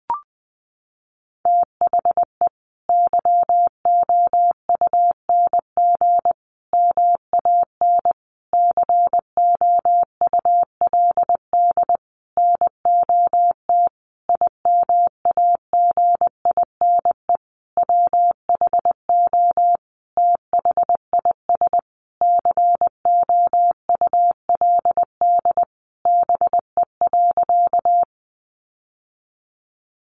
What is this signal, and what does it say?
Morse code, heard as a keyed tone, about 20 wpm, 700 Hz. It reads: THE YOUNG MAN COULD NOT IMAGINE WHO THIS COULD BE.
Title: Sentences from Top 1400 Words 20wpm